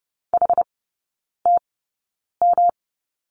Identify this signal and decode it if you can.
Morse code, heard as a keyed tone, about 30 wpm, spaced as for 9 wpm, 700 Hz. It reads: HTM
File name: Lesson 18e - Letter m - Three Characters 30wpm